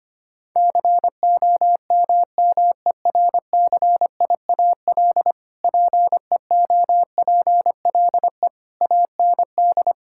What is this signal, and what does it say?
Morse code, heard as a keyed tone, about 25 wpm, 700 Hz. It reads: COMMERCIAL PEOPLE AND
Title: Sentences from Top 1000 Words - Review Part 5 - 25wpm